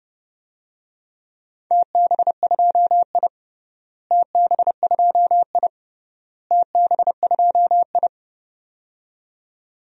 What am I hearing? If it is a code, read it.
Morse code, heard as a keyed tone, about 32 wpm, 700 Hz. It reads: TB2I TB2I TB2I
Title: DXCC-100 Call Signs - 4 Characters - Repeated 3x - 30wpm